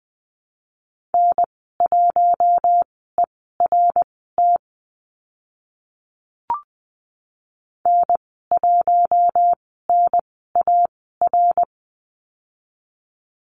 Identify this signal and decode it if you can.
Morse code, heard as a keyed tone, about 20 wpm, spaced as for 15 wpm, 700 Hz. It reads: N1ERT N1NAR
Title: 055 Lesson 10g - Letter r - Call Signs 20wpm